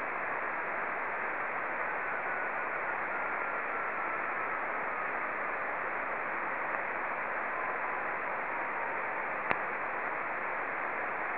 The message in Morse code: none